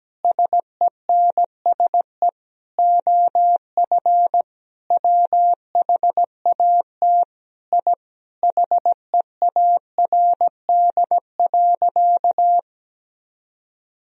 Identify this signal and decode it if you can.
Morse code, heard as a keyed tone, about 17 wpm, 700 Hz. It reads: SENSE OF WHAT I HEARD.
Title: Sentences from Top 1000 Words - Review Part 4 - 17wpm